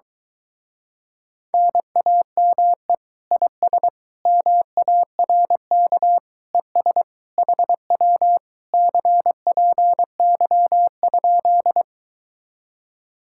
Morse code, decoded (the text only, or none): NAME IS MARK ES HW CPY?